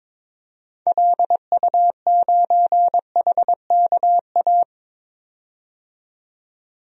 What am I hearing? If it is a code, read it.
Morse code, heard as a keyed tone, about 22 wpm, 700 Hz. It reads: LU9HKA
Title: DXCC-100 Call Signs - 6 Characters - 22wpm